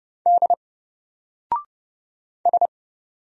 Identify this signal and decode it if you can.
Morse code, heard as a keyed tone, about 30 wpm, 700 Hz. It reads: D S
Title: Single Character (Letter:Number:Punctuation:Prosign) 30wpm